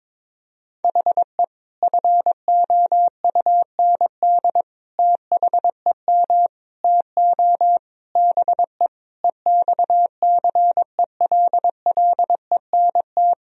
Morse code, heard as a keyed tone, about 22 wpm, 700 Hz. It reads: HE FOUND THEM TO BE EXCELLENT
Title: Sentences from Top 1000 Words - Review Part 3 - 22wpm